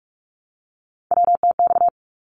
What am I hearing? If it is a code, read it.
Morse code, heard as a keyed tone, about 44 wpm, 700 Hz. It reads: WTX
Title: ARRL Sections 45wpm